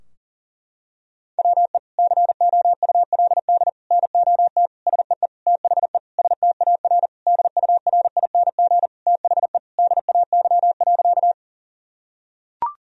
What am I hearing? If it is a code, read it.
Morse code, heard as a keyed tone, about 40 wpm, 700 Hz. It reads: WE COULD NOT SEE THE STAR DURING THE DAY.
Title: Sentences from Top 1000 Words - Review Part 2 - 40wpm